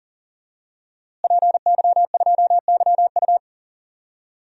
Morse code, heard as a keyed tone, about 40 wpm, 700 Hz. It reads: PY2YU
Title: DXCC-100 Call Signs - 5 Characters - Repeated 3x - 40wpm